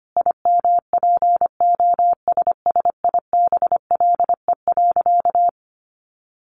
Morse code, heard as a keyed tone, about 25 wpm, 700 Hz. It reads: IMPOSSIBLE.
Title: Sentences from Top 1300 Words 25wpm